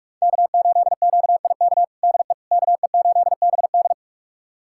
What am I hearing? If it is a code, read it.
Morse code, heard as a keyed tone, about 41 wpm, 700 Hz. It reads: K8QIK DE KE8BD
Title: USA Callsign DE Callsign - 5 Characters - 45wpm